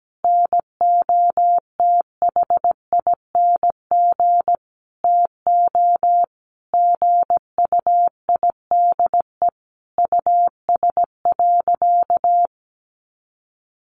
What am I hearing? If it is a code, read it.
Morse code, heard as a keyed tone, about 17 wpm, 700 Hz. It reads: NOTHING TO GUIDE US.
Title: Sentences from Top 800 Words 17wpm